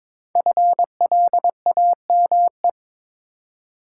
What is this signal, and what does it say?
Morse code, heard as a keyed tone, about 22 wpm, 700 Hz. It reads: FLAME
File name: 5 Letter Words - First 3 Letters Given - 22wpm